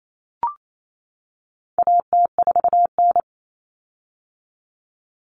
Morse code, heard as a keyed tone, about 28 wpm, 700 Hz. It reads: AT4N
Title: Common Cut Numbers - 4-Digit 28wpm